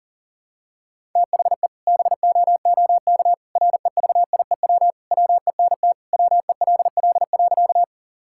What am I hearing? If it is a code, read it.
Morse code, heard as a keyed tone, about 40 wpm, 700 Hz. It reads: THE BOOK REVIEW WENT WELL.
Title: Sentences from Top 1000 Words - Review Part 6 - 40wpm